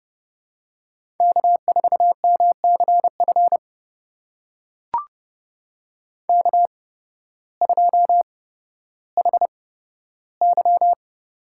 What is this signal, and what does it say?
Morse code, heard as a keyed tone, about 30 wpm, spaced as for 14 wpm, 700 Hz. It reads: K4MCF K2HY
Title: USA Call Signs - 5 Characters - 8x Character Spacing - 30wpm